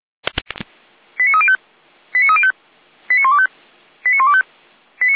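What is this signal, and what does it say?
Five-tone selective call (ZVEI), heard as quick runs of five tones, tones of about 80 ms. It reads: 89305 89305 89125 89125
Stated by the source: ZVEI five-tone call received on SDR by Feuerwehr Orlamünde (audio.f85360.feuerwehr-orlamuende.org).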